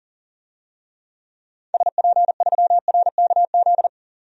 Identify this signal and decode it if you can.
Morse code, heard as a keyed tone, about 40 wpm, 700 Hz. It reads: SP3RKZ
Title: DXCC-100 Call Signs - 6 Characters - 40wpm